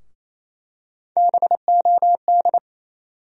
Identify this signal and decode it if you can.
Morse code, heard as a keyed tone, about 28 wpm, 700 Hz. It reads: BOD
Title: USA Call Sign Postfixes - 3 Letters 28wpm